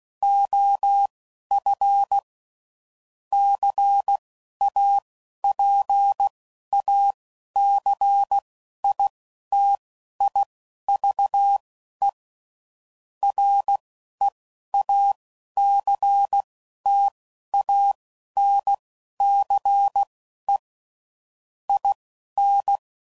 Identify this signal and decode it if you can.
Morse code, heard as a keyed tone, about 16 wpm, spaced as for 11 wpm, 800 Hz. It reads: OF CAPACITIVE REACTANCE IN
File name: CW Code Groups @10WPM / Technical Subjects / XC